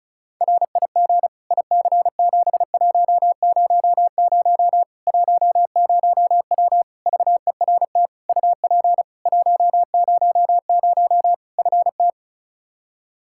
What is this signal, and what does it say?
Morse code, heard as a keyed tone, about 36 wpm, 700 Hz. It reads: RIG IC7100 10W VERT UP 100 FT
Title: Practice QSO - Exchange 2 - Info - 35wpm